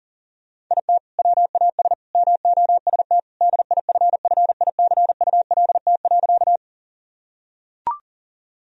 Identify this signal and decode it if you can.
Morse code, heard as a keyed tone, about 40 wpm, 700 Hz. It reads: IT WAS MOST DIFFICULT.
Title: Sentences from Top 1000 Words - Review Part 1 - 40wpm